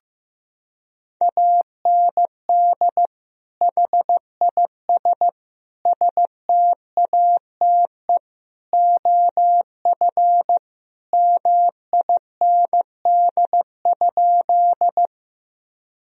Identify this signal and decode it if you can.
Morse code, heard as a keyed tone, about 14 wpm, 700 Hz. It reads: AND HIS STATE OF MIND?
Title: Sentences from Top 1000 Words - Review Part 2 - 15wpm